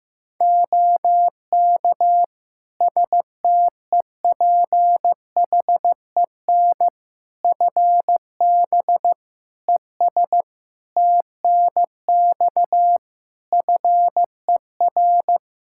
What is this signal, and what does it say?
Morse code, heard as a keyed tone, about 15 wpm, 700 Hz. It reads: OK STEPHEN FB ES TNX FER
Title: Practice QSO - Exchange 2 - Info - 15wpm